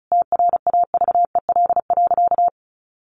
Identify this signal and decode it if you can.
Morse code, heard as a keyed tone, about 34 wpm, 700 Hz. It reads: TRAVEL.
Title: Sentences from Top 1300 Words 35wpm